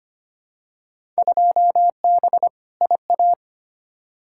Morse code, heard as a keyed tone, about 25 wpm, 700 Hz. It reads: 2B IA
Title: Field Day Exchanges Encore - 25wpm